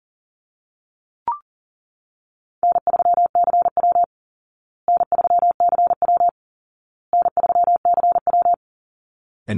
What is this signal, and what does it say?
Morse code, heard as a keyed tone, about 37 wpm, 700 Hz. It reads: N3CW N3CW N3CW
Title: Easy Call Signs Repeated 3x - 40wpm